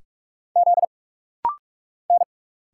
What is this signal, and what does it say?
Morse code, heard as a keyed tone, about 45 wpm, 690 Hz. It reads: Z N
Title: Single Letters - Rapid-Fire 45wpm